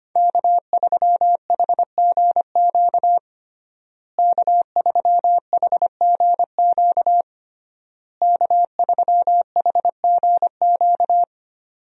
Morse code, heard as a keyed tone, about 25 wpm, 700 Hz. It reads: K3HGQ K3HGQ K3HGQ
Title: USA Call Signs - 5 Characters - Repeated 3x - 25wpm